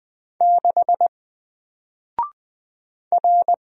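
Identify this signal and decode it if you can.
Morse code, heard as a keyed tone, about 20 wpm, 700 Hz. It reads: 6 R
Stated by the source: Single letter and number Morse drill at 20 wpm, from N4LCD.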